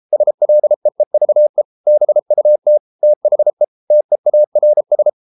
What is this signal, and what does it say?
Morse code, heard as a keyed tone, about 33 wpm, 600 Hz. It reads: SLEEVE BUT THE TEARS